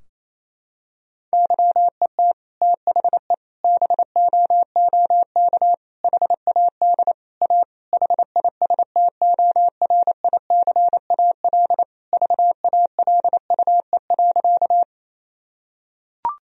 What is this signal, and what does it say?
Morse code, heard as a keyed tone, about 28 wpm, 700 Hz. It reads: YET THE BOOK HAD A HISTORICAL VALUE.
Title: Sentences from Top 1000 Words - Review Part 4 - 28wpm